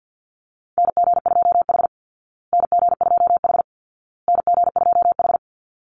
Morse code, heard as a keyed tone, about 50 wpm, 700 Hz. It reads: NG2H NG2H NG2H
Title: Call Signs Repeated 2x - 50wpm